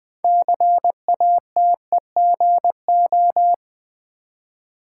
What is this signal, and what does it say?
Morse code, heard as a keyed tone, about 20 wpm, 700 Hz. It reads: CATEGO
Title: Build A Word - 2 Letters At A Time - 8-Letter Words - 20wpm